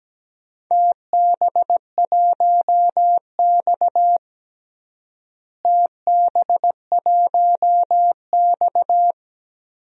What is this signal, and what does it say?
Morse code, heard as a keyed tone, about 17 wpm, 700 Hz. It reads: TB1X TB1X
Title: DXCC-100 Call Signs - 4 Characters - Repeated 2x - 17wpm